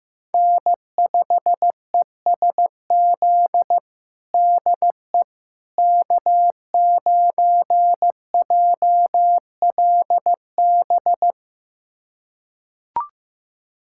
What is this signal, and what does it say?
Morse code, heard as a keyed tone, about 16 wpm, 700 Hz. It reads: N5ESZ DE K9JLB
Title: USA Callsign DE Callsign - 5 Characters - 15wpm